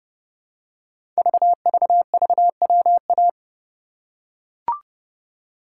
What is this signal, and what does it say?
Morse code, heard as a keyed tone, about 29 wpm, 700 Hz. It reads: VVVWA
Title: Code Groups - 5 Letters - 30wpm